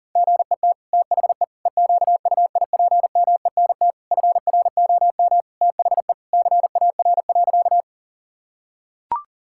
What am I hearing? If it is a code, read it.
Morse code, heard as a keyed tone, about 37 wpm, 700 Hz. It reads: GET THE EQUIPMENT FROM THE CAR.
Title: Sentences from Top 1000 Words - Review Part 5 - 40wpm